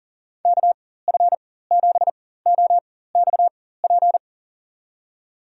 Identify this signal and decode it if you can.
Morse code, heard as a keyed tone, about 40 wpm, spaced as for 19 wpm, 700 Hz. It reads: KF7OXP
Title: USA Call Signs - 6 Characters - 4x Character Spacing - 40wpm